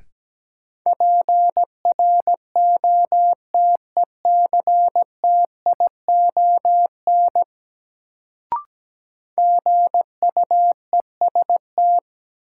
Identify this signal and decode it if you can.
Morse code, heard as a keyed tone, about 17 wpm, 700 Hz. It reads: PROTECTION GUEST
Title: Sentences from Top 1600 Words - New Word Drill - 17wpm